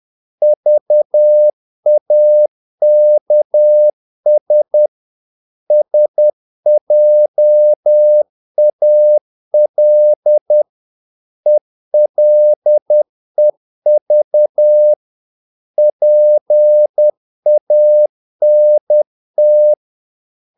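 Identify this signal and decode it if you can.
Morse code, heard as a keyed tone, about 10 wpm, 600 Hz. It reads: VAKS SJAL ELEV PANT
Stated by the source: Morse practice of short Danish words at 10 wpm, from CW med Gnister.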